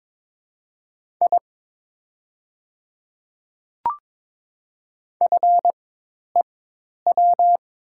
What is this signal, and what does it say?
Morse code, heard as a keyed tone, about 22 wpm, spaced as for 10 wpm, 690 Hz. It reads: I FEW
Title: 4x Character Spacing - Top 300 Words - 22wpm